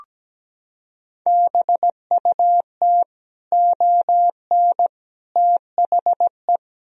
Morse code, heard as a keyed tone, about 17 wpm, 700 Hz. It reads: BUT ON THE
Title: Sentences from Top 1000 Words - Review Part 4 - 17wpm